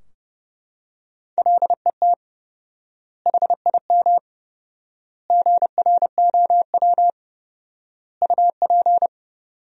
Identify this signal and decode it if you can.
Morse code, heard as a keyed tone, about 30 wpm, 700 Hz. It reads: LET HIM GROW UP
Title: 4x Word-Spacing - Sentences from Top 300 Words - 30wpm